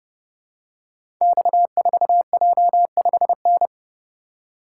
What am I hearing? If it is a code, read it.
Morse code, heard as a keyed tone, about 30 wpm, 700 Hz. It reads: X4J5N
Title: Code Groups - 5 Letter:Numbers - 30wpm